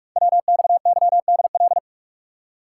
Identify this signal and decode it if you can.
Morse code, heard as a keyed tone, about 44 wpm, 690 Hz. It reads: WXYDL
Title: Code Groups - 5 Letters - 45wpm